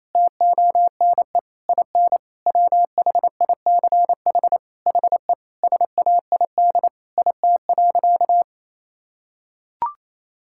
Morse code, heard as a keyed tone, about 28 wpm, 700 Hz. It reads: TONE IN WHICH HE SAID IT.
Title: Sentences from Top 1600 Words 28wpm